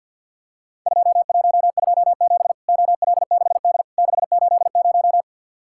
Morse code, heard as a keyed tone, about 51 wpm, 700 Hz. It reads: 2127 OL6D 680
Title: Worked All Europe DX Contest - QTC Exchange - 50wpm